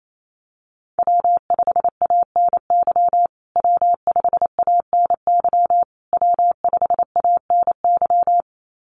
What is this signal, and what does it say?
Morse code, heard as a keyed tone, about 28 wpm, 700 Hz. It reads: W5ANY W5ANY W5ANY
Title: USA Call Signs Repeated 3x 28wpm